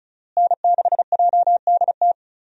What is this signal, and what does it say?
Morse code, heard as a keyed tone, about 35 wpm, 700 Hz. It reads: N6JDT
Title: USA Call Signs - 5 Characters - Repeated 3x - 35wpm